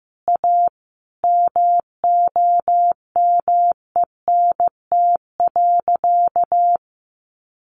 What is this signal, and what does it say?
Morse code, heard as a keyed tone, about 15 wpm, 700 Hz. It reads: A MOMENT.
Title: Sentences from Top 1400 Words 15wpm